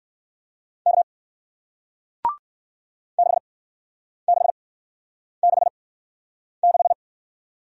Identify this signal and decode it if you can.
Morse code, heard as a keyed tone, about 54 wpm, spaced as for 9 wpm, 700 Hz. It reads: K 6666
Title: Speed-Racing Single Letter-Number 45wpm